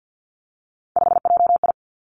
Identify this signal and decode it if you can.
Morse code, heard as a keyed tone, about 50 wpm, 700 Hz. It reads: 5JI
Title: Three Challenging Letter-Number Combinations 50wpm